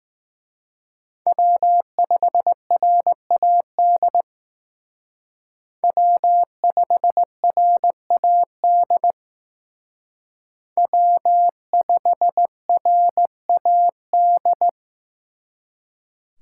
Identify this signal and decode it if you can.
Morse code, heard as a keyed tone, about 18 wpm, 700 Hz. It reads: W5RAD W5RAD W5RAD
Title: Speed Racing - USA 1x3 Call Signs 15wpm